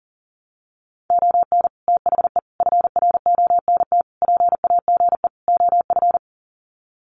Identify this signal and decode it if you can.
Morse code, heard as a keyed tone, about 40 wpm, 700 Hz. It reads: ON THE FRONT PAGE OF
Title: Sets of 5 Words Encore 40wpm